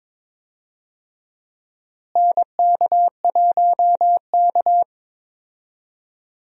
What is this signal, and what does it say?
Morse code, heard as a keyed tone, about 22 wpm, 700 Hz. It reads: NK1K
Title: USA Call Signs - 4 Characters - Repeated 2x - 22wpm